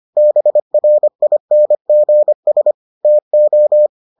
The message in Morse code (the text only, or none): BRINGS TO